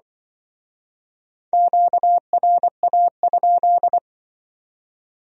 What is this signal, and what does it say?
Morse code, heard as a keyed tone, about 24 wpm, 700 Hz. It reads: QRA?
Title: Speed-Racing Q-Codes - 20wpm